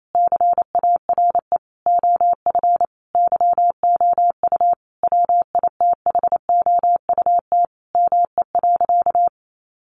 Morse code, heard as a keyed tone, about 28 wpm, 700 Hz. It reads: CARE OF YOU WITHOUT ME.
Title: Sentences from Top 1200 Words 28wpm